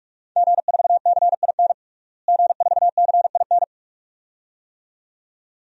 Morse code, heard as a keyed tone, about 41 wpm, 700 Hz. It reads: G4CIN G4CIN
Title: DXCC-100 Call Signs - 5 Characters - Repeated 2x - 45wpm